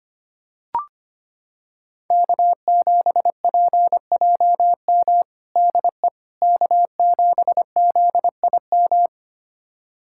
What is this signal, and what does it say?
Morse code, heard as a keyed tone, about 25 wpm, 700 Hz. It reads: K7PJM DE K7ZIM
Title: USA Callsign DE Callsign - 5 Characters - 25wpm